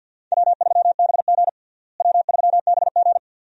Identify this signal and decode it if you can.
Morse code, heard as a keyed tone, about 52 wpm, 700 Hz. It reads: W3BG W3BG
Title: USA Call Signs - 4 Characters - Repeated 2x - 50wpm